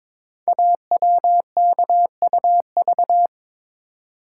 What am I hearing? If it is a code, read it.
Morse code, heard as a keyed tone, about 23 wpm, 700 Hz. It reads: AWKUV
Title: Code Groups - 5 Letters - 22wpm